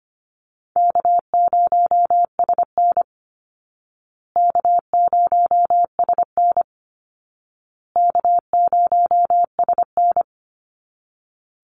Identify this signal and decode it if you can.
Morse code, heard as a keyed tone, about 25 wpm, 700 Hz. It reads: K0SN K0SN K0SN
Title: Call Signs Repeated 2x - 25wpm